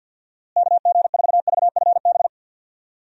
Morse code, heard as a keyed tone, about 51 wpm, 700 Hz. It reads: KG4VFB
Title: USA Call Signs - 6 Characters - Repeated 2x - 50wpm